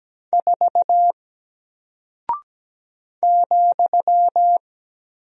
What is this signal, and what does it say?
Morse code, heard as a keyed tone, about 17 wpm, 700 Hz. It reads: 4 ,
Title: Single Character (Letter:Number:Punctuation:Prosign) Alt 17wpm